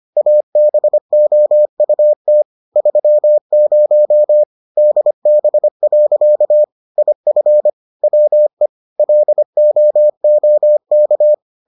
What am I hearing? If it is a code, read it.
Morse code, heard as a keyed tone, about 25 wpm, 600 Hz. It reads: ABOUT 30 DB. IF WE LOOK